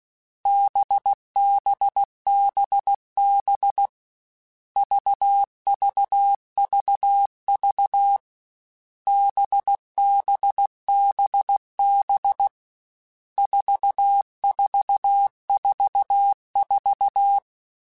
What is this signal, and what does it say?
Morse code, heard as a keyed tone, about 16 wpm, 800 Hz. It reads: BBBB VVVV BBBB 4444
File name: Groups@13 / Groups@13 mp3s / VB46@13